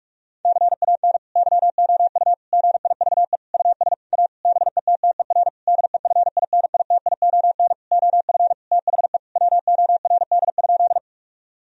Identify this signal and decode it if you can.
Morse code, heard as a keyed tone, about 45 wpm, 700 Hz. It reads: CAN YOU GIVE US A BETTER DEFINITION OF THE WORD?
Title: Sentences from Top 1000 Words - Review Part 4 - 45wpm